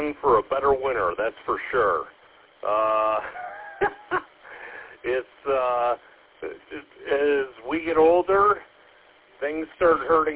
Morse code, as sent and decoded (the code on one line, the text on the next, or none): none